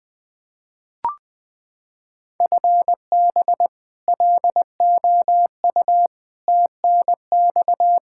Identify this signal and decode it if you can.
Morse code, heard as a keyed tone, about 21 wpm, 700 Hz. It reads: FB LOU TNX
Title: K1USN SST QSO Practice 20wpm